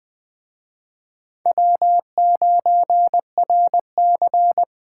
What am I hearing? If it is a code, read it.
Morse code, heard as a keyed tone, about 21 wpm, 700 Hz. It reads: W9RC
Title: Speed Racing - USA 1x2 Call Signs 17wpm